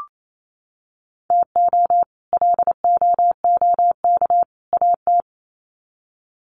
Morse code, heard as a keyed tone, about 28 wpm, 700 Hz. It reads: TO LOOK AT
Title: Sets of 3 Words 28wpm